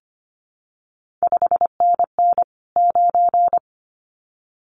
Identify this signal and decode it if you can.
Morse code, heard as a keyed tone, about 25 wpm, 700 Hz. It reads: 5NN 9
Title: CQ WW DX Contest Exchange - 25wpm